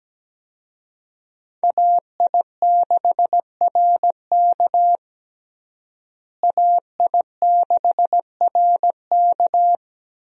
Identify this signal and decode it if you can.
Morse code, heard as a keyed tone, about 17 wpm, 700 Hz. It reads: AI6RK AI6RK
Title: USA Call Signs - 5 Characters - Repeated 3x - 17wpm